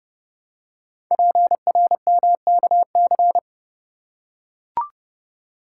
Morse code, heard as a keyed tone, about 30 wpm, 700 Hz. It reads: PRMKC
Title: Code Groups - 5 Letters - 30wpm